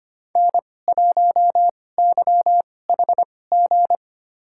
Morse code, heard as a keyed tone, about 25 wpm, spaced as for 18 wpm, 700 Hz. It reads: N1YHG
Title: USA Call Signs - 5 Characters - 2x Character Spacing - 25wpm